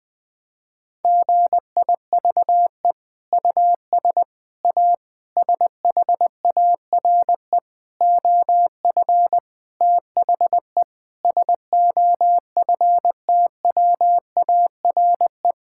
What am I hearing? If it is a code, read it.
Morse code, heard as a keyed tone, about 21 wpm, 700 Hz. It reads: GIVE US A SHARE OF THE SOFTWARE
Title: Sentences from Top 1000 Words - Review Part 6 - 20wpm